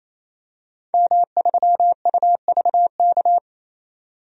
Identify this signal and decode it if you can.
Morse code, heard as a keyed tone, about 28 wpm, 700 Hz. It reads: M3UVK
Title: DXCC-100 Call Signs - 5 Characters - 28wpm